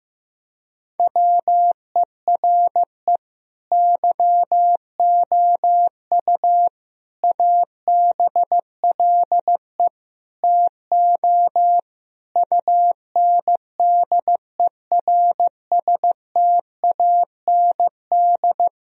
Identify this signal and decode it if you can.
Morse code, heard as a keyed tone, about 16 wpm, 700 Hz. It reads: WERE YOU ABLE TO UNDERSTAND
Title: Sentences from Top 1000 Words - Review Part 2 - 15wpm